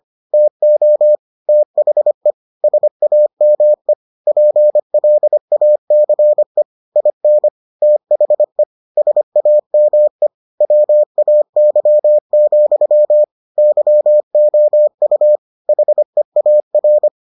Morse code, heard as a keyed tone, about 25 wpm, 600 Hz. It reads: TO THE SAME PLACE IN THE SAME WAY, YOU HEAR